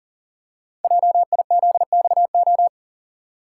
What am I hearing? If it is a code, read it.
Morse code, heard as a keyed tone, about 42 wpm, 690 Hz. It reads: JIZXO
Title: Code Groups - 5 Letters - 40wpm